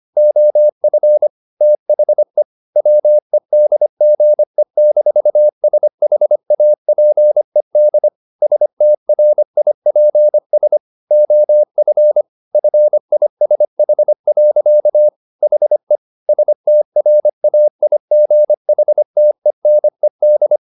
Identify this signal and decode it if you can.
Morse code, heard as a keyed tone, about 25 wpm, 600 Hz. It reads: OF THE WEDGE-SHAPED STRIPS OF FISH. HE STRAIGHTENED
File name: the old man and the sea part 1 25wpm